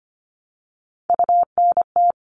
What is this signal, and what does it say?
Morse code, heard as a keyed tone, about 25 wpm, 700 Hz. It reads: UNT
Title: All Cut Numbers - 3-Digit 25wpm